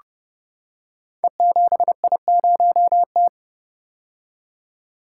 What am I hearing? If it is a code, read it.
Morse code, heard as a keyed tone, about 30 wpm, 700 Hz. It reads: E7I0T
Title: Code Groups - 5 Letter:Numbers - 30wpm